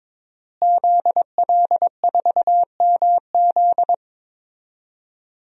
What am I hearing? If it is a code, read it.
Morse code, heard as a keyed tone, about 22 wpm, 700 Hz. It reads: ZL4MZ
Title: DXCC-100 Call Signs - 5 Characters - Repeated 2x - 22wpm